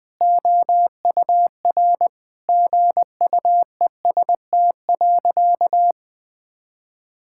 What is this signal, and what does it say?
Morse code, heard as a keyed tone, about 20 wpm, 700 Hz. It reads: OUR GUEST.
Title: Sentences from Top 1600 Words 20wpm